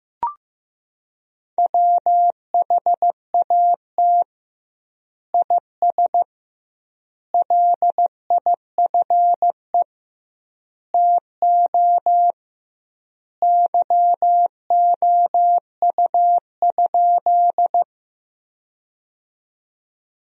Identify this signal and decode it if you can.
Morse code, heard as a keyed tone, about 15 wpm, 700 Hz. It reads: WHAT IS LIFE TO YOU?
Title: 2x Word-Spacing - Sentences from Top 300 Words - 15wpm